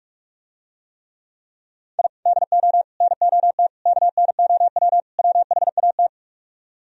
Morse code, heard as a keyed tone, about 48 wpm, 700 Hz. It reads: I DO NOT KNOW WHAT